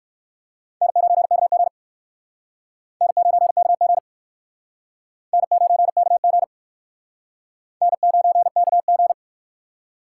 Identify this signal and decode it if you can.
Morse code, heard as a keyed tone, about 58 wpm, 700 Hz. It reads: N9KG N9KG N9KG N9KG